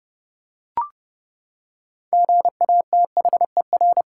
Morse code, heard as a keyed tone, about 32 wpm, 700 Hz. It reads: GATHER
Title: Six Letter Words - 30wpm